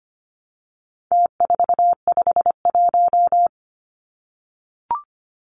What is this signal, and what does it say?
Morse code, heard as a keyed tone, about 25 wpm, 700 Hz. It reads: T451
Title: Numbers - Contest Style - 4-Digit 25wpm